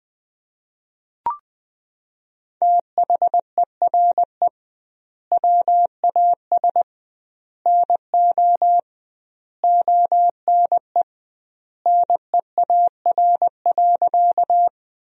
Morse code, heard as a keyed tone, about 20 wpm, 700 Hz. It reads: THERE WAS NO ONE NEAR.
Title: 2x Word-Spacing - Sentences from Top 200 Words - 20wpm